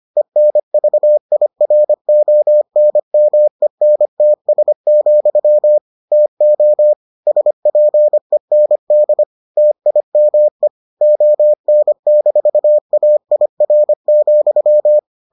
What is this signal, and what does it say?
Morse code, heard as a keyed tone, about 25 wpm, 600 Hz. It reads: ENVIRONMENTS, TO SPEND TIME ON-AIR,